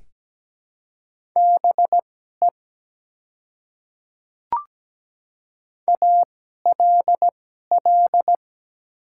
Morse code, heard as a keyed tone, about 17 wpm, spaced as for 12 wpm, 700 Hz. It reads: BE ALL
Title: 2x Character Spacing - Top 100 Words - 17wpm